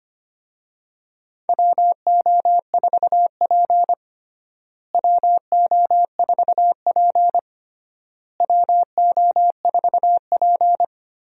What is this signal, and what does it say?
Morse code, heard as a keyed tone, about 25 wpm, 700 Hz. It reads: WO4P WO4P WO4P
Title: USA Call Signs - 4 Characters - Repeated 3x - 25wpm